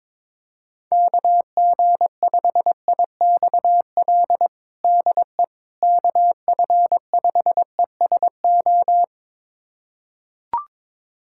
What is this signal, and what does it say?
Morse code, heard as a keyed tone, about 22 wpm, 700 Hz. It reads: KG5IXL DE KF5ESO
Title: USA Callsign DE Callsign - 6 Characters - 22wpm